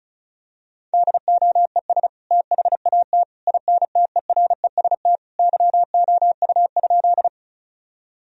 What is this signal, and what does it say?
Morse code, heard as a keyed tone, about 36 wpm, 700 Hz. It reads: DOES THAT INTEREST YOU?
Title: Sentences from Top 1000 Words - Review Part 2 - 35wpm